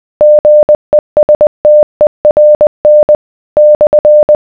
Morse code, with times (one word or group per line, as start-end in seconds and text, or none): 0.2-3.2 GESTERN
3.6-4.4 /